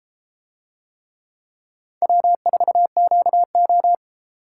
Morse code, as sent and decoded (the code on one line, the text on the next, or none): .-- ....- --.- ---
W4QO